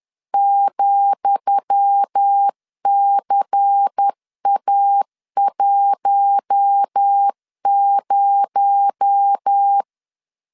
Morse code, heard as a keyed tone, about 10 wpm, 800 Hz. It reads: ,CA10
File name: LEZIONI / 10 SIGNS / 10D COMA 10wpm 3013